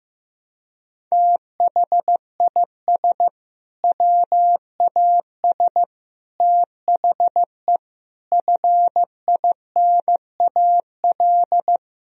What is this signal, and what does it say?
Morse code, heard as a keyed tone, about 15 wpm, 700 Hz. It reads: THIS WAS THE FINAL
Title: Sentences from Top 1000 Words - Review Part 2 - 15wpm